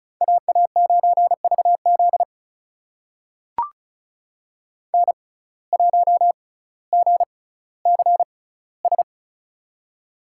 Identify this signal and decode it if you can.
Morse code, heard as a keyed tone, about 36 wpm, spaced as for 18 wpm, 700 Hz. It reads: AA9VZ N1GCS
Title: USA Call Signs - 6x Character Spacing - 35wpm